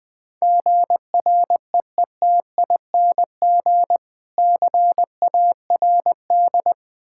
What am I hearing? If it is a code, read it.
Morse code, heard as a keyed tone, about 20 wpm, 700 Hz. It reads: GREETING CARD
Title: Difficult Words with Hints - 20wpm